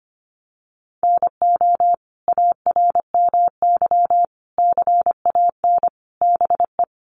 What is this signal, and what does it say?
Morse code, heard as a keyed tone, about 25 wpm, 700 Hz. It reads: NO ARMY CAN BE
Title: Sentences from Top 1300 Words 25wpm